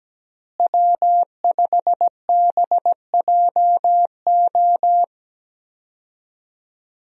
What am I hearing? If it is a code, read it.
Morse code, heard as a keyed tone, about 17 wpm, 700 Hz. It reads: W5BJO
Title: USA Call Signs - 5 Characters - Repeated 3x - 17wpm